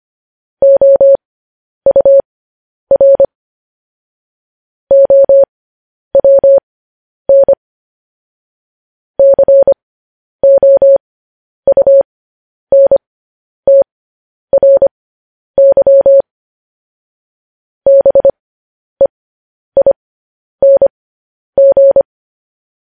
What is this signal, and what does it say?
Morse code, heard as a keyed tone, about 25 wpm, spaced as for 10 wpm, 550 Hz. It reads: OUR OWN COUNTRY BEING
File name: podcast / News headlines-1x-25x10